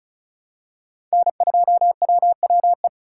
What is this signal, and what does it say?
Morse code, heard as a keyed tone, about 36 wpm, 700 Hz. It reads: N2WWE
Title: USA Call Signs - 5 Characters - 3x Character Spacing - 35wpm